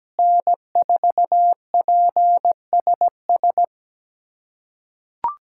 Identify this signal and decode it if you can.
Morse code, heard as a keyed tone, about 17 wpm, 700 Hz. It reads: N4PSS
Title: USA Callsign DE Callsign - 5 Characters - 17wpm